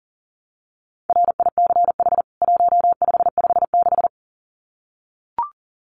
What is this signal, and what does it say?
Morse code, heard as a keyed tone, about 40 wpm, 700 Hz. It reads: RICH 1556
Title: CWT Exchange 40wpm